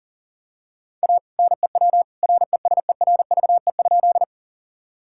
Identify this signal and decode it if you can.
Morse code, heard as a keyed tone, about 38 wpm, 700 Hz. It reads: A NEW RESERVE?
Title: Sentences from Top 1000 Words - Review Part 6 - 40wpm